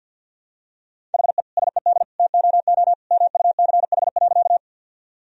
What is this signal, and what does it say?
Morse code, heard as a keyed tone, about 51 wpm, 700 Hz. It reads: HE SED TOO MUCH.
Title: Sentences from Top 200 Words and Abbreviations 50wpm